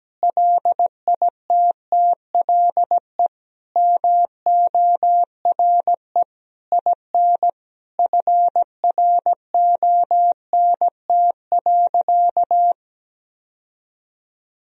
Morse code, heard as a keyed tone, about 17 wpm, 700 Hz. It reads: LITTLE MORE IN FRONT.
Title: Sentences from Top 1000 Words - Review Part 2 - 17wpm